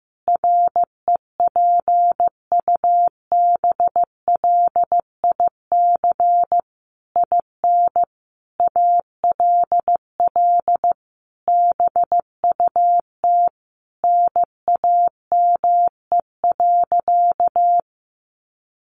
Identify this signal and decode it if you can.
Morse code, heard as a keyed tone, about 15 wpm, 700 Hz. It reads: REPUBLIC IN ALL BUT NAME.
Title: Sentences from Top 1500 Words 15wpm